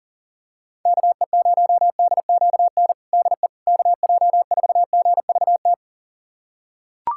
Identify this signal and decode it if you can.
Morse code, heard as a keyed tone, about 40 wpm, 700 Hz. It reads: KE0DQN DE KJ4GVT
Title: USA Callsign DE Callsign - 6 Characters - 40wpm